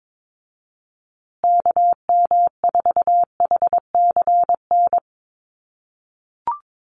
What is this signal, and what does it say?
Morse code, heard as a keyed tone, about 22 wpm, 700 Hz. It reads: KM4HCN
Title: USA Call Signs Repeated 2x 22wpm